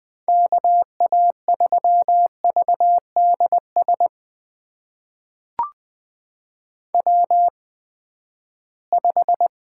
Morse code, heard as a keyed tone, about 20 wpm, spaced as for 14 wpm, 700 Hz. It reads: KA3VDS W5
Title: USA Call Signs - 8x Character Spacing - 20wpm